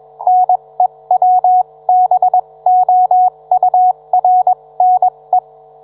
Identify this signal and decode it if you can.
Morse code, heard as a keyed tone, about 21 wpm, spaced as for 18 wpm, 730 Hz. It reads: NEWBOURNE
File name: book